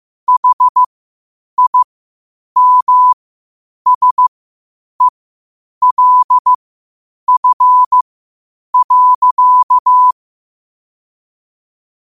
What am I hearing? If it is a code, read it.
Morse code, heard as a keyed tone, about 15 wpm, spaced as for 8 wpm, 1000 Hz. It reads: HIMSELF.
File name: qotdfiles / 8wpm